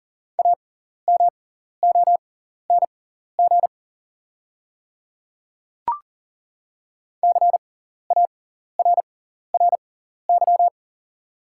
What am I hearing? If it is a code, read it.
Morse code, heard as a keyed tone, about 40 wpm, spaced as for 14 wpm, 700 Hz. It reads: AMONG CARRY
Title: 6x Character Spacing - Top 500 Words - 40wpm